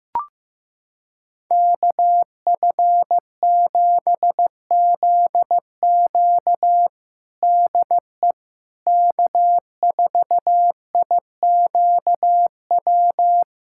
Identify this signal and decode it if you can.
Morse code, heard as a keyed tone, about 15 wpm, 700 Hz. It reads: KF7ZQ DE K4IQW
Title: USA Callsign DE Callsign - 5 Characters - 15wpm